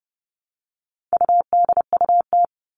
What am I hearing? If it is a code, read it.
Morse code, heard as a keyed tone, about 30 wpm, 700 Hz. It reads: UDUT
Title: All Cut Numbers - 4-Digit 30wpm